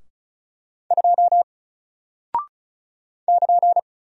Morse code, heard as a keyed tone, about 36 wpm, 700 Hz. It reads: 2 <KN>
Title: Single Character (Letter:Number:Punctuation:Prosign) Alt 35wpm